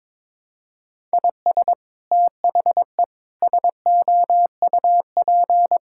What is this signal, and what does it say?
Morse code, heard as a keyed tone, about 23 wpm, 700 Hz. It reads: IS THE SOUP
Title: Sentences from Top 1000 Words - Review Part 6 - 22wpm